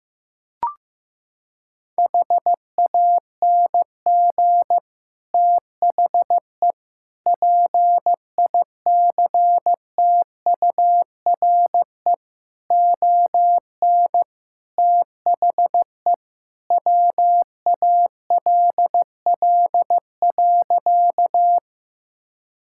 Morse code, heard as a keyed tone, about 15 wpm, 700 Hz. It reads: HANG THE PICTURE ON THE WALL.
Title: Sentences from Top 1600 Words 15wpm